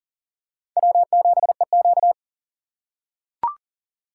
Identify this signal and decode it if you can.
Morse code, heard as a keyed tone, about 37 wpm, 700 Hz. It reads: W7EQ
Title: USA Call Signs - 4 Characters - 2x Character Spacing - 40wpm